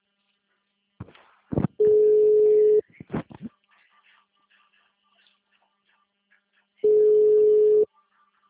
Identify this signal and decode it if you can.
Ringback tone (Europe, 425 Hz).